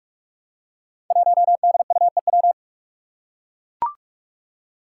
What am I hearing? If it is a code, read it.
Morse code, heard as a keyed tone, about 44 wpm, 700 Hz. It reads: 1DUEW
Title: Five Challenging Letter-Number Combinations 45wpm